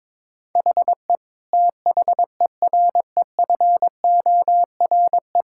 Morse code, heard as a keyed tone, about 22 wpm, 700 Hz. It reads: HE THEREFORE
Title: Sentences from Top 1000 Words - Review Part 5 - 22wpm